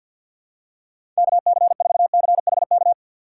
Morse code, heard as a keyed tone, about 50 wpm, 700 Hz. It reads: KC4CHK
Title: USA Call Signs - 6x Character Spacing - 50wpm